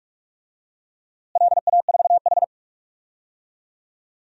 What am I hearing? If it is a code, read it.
Morse code, heard as a keyed tone, about 45 wpm, 700 Hz. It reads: LA4H
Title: DXCC-100 Call Signs - 4 Characters - Repeated 2x - 45wpm